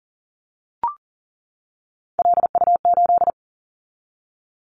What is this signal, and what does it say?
Morse code, heard as a keyed tone, about 40 wpm, 700 Hz. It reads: LU8
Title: Callsign Prefixes for Top 50 DXCC - 40wpm